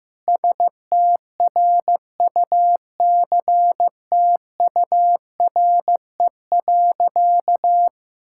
Morse code, heard as a keyed tone, about 15 wpm, 700 Hz. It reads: STRUCTURE.
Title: Sentences from Top 1000 Words - Review Part 6 - 15wpm